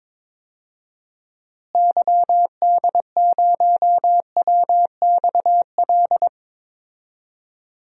Morse code, heard as a keyed tone, about 21 wpm, 700 Hz. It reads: YD0WXL
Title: DXCC-100 Call Signs - 6 Characters - Repeated 2x - 22wpm